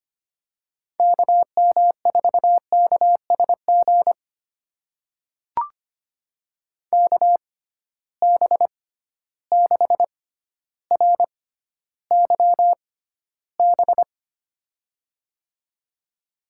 Morse code, heard as a keyed tone, about 25 wpm, spaced as for 13 wpm, 700 Hz. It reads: KM4KSG KB6RYB
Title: USA Call Signs - 6 Characters - 6x Character Spacing - 25wpm